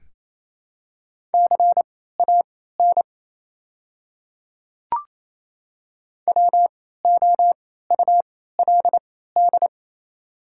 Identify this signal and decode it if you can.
Morse code, heard as a keyed tone, about 28 wpm, spaced as for 16 wpm, 700 Hz. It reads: CAN WOULD
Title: 3x Character Spacing - Top 100 Words - 28wpm